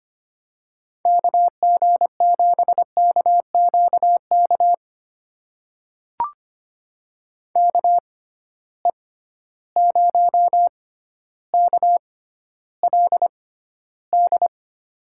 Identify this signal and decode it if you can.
Morse code, heard as a keyed tone, about 25 wpm, spaced as for 13 wpm, 700 Hz. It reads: KG7KQK KE0KLD
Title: USA Call Signs - 6 Characters - 6x Character Spacing - 25wpm